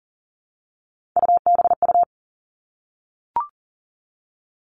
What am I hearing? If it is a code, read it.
Morse code, heard as a keyed tone, about 38 wpm, 700 Hz. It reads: UBU
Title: Three Challenging Letter-Number Combinations 40wpm